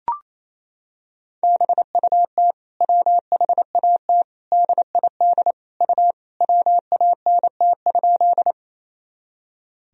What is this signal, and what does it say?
Morse code, heard as a keyed tone, about 28 wpm, 700 Hz. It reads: BUT WHAT DID U WANT?
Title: Sentences from Top 200 Words and Abbreviations 28wpm